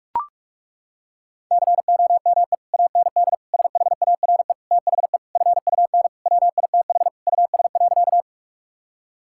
Morse code, heard as a keyed tone, about 41 wpm, 700 Hz. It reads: COME AND SHARE THE FUN WITH US.
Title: Sentences from Top 1000 Words - Review Part 6 - 45wpm